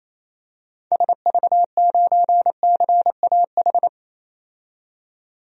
Morse code, heard as a keyed tone, about 28 wpm, 700 Hz. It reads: SV9CAH
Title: DXCC-100 Call Signs - 6 Characters - 28wpm